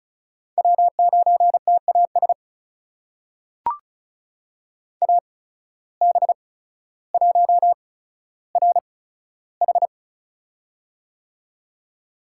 Morse code, heard as a keyed tone, about 36 wpm, spaced as for 15 wpm, 700 Hz. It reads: W9TAS AB1RH
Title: USA Call Signs - 5 Characters - 8x Character Spacing - 35wpm